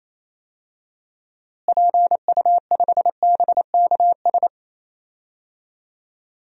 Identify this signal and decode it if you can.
Morse code, heard as a keyed tone, about 28 wpm, 700 Hz. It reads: PU5BKS
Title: DXCC-100 Call Signs - 6 Characters - Repeated 3x - 28wpm